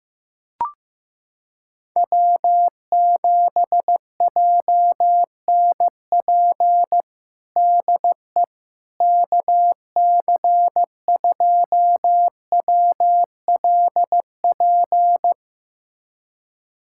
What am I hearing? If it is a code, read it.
Morse code, heard as a keyed tone, about 15 wpm, 700 Hz. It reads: W7JNP DE KC2WLP
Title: USA Callsign DE Callsign 15wpm